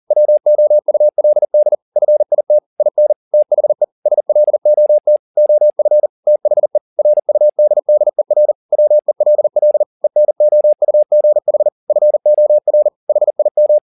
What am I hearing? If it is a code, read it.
Morse code, heard as a keyed tone, about 40 wpm, 600 Hz. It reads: WOULD FIT IN THE SLOT OF THE RUDDER WELL ENOUGH FOR HIM
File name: the old man and the sea 2 40wpm